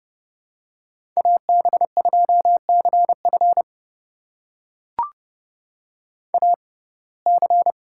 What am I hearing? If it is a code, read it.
Morse code, heard as a keyed tone, about 30 wpm, spaced as for 22 wpm, 700 Hz. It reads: AB2CF AC
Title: USA Call Signs - 5 Characters - 6x Character Spacing - 30wpm